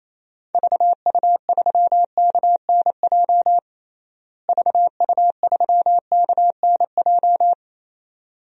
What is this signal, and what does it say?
Morse code, heard as a keyed tone, about 28 wpm, 700 Hz. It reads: VU3KNJ VU3KNJ
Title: DXCC-100 Call Signs - 6 Characters - Repeated 2x - 28wpm